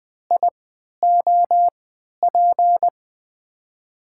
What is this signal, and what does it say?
Morse code, heard as a keyed tone, about 21 wpm, spaced as for 12 wpm, 700 Hz. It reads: IOP
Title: USA Call Signs - 6 Characters - 3x Character Spacing - 20wpm